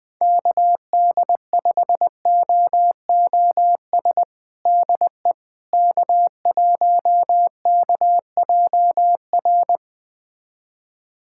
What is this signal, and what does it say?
Morse code, heard as a keyed tone, about 21 wpm, 700 Hz. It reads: KD5OOS DE K1KJR
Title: USA Callsign DE Callsign 20wpm